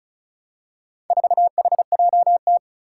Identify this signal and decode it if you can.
Morse code, heard as a keyed tone, about 36 wpm, 700 Hz. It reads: 4HJT
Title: Four Challenging Letter-Number Combinations 35wpm